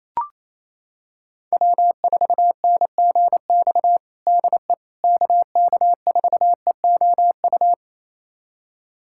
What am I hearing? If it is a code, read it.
Morse code, heard as a keyed tone, about 28 wpm, 700 Hz. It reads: W4NGX DE KK4EOU
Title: USA Callsign DE Callsign 28wpm